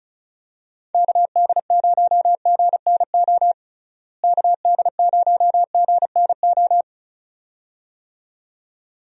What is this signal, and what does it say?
Morse code, heard as a keyed tone, about 36 wpm, 700 Hz. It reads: KD0GNO KD0GNO
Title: USA Call Signs - 6 Characters - Repeated 2x - 35wpm